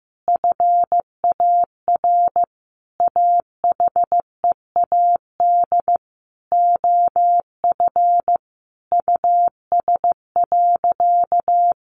Morse code, heard as a keyed tone, about 15 wpm, 700 Hz. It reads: FAR AHEAD OF US.
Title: Sentences from Top 1100 Words 15wpm